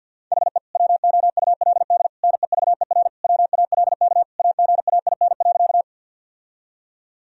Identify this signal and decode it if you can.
Morse code, heard as a keyed tone, about 47 wpm, 700 Hz. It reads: HE WOULD NEVER WALK AGAIN.
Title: Sentences from Top 1000 Words - Review Part 1 - 50wpm